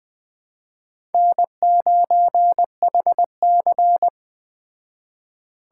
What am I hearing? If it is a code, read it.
Morse code, heard as a keyed tone, about 20 wpm, 700 Hz. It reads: N9HC
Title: Speed Racing - USA Call Signs 17wpm